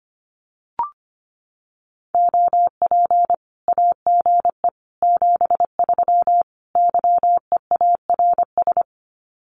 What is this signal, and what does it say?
Morse code, heard as a keyed tone, about 25 wpm, 700 Hz. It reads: OP AGE 73 YEARS
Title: Common QSO Phrases 25wpm